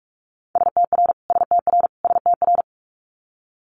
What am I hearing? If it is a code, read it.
Morse code, heard as a keyed tone, about 46 wpm, 700 Hz. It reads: STR STR STR
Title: Multi-letter Phonemes Repeated 3x 45wpm